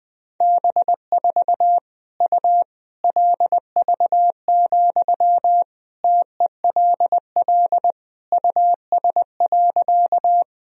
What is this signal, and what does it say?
Morse code, heard as a keyed tone, about 20 wpm, 700 Hz. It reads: B4 U LV, TELL US.
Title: Sentences from Top 400 Words and Abbreviations 20wpm